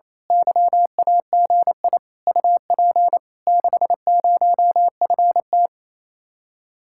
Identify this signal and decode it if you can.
Morse code, heard as a keyed tone, about 27 wpm, 700 Hz. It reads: YAGI UP 60FT
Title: Speed-Racing Common QSO Phrases Encore - 28wpm